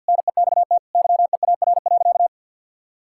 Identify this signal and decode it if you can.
Morse code, heard as a keyed tone, about 52 wpm, 700 Hz. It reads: NEXT YEAR.
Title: Sentences from Top 1000 Words - Review Part 6 - 50wpm